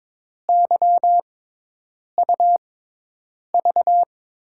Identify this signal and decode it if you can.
Morse code, heard as a keyed tone, about 22 wpm, spaced as for 8 wpm, 700 Hz. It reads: YUV
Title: USA Call Signs - 5 Characters - 6x Character Spacing - 22wpm